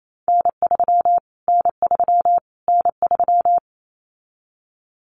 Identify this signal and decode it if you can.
Morse code, heard as a keyed tone, about 28 wpm, 700 Hz. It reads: N3 N3 N3
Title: Callsign Prefixes for Top 25 DXCC - Repeated 3x - 28wpm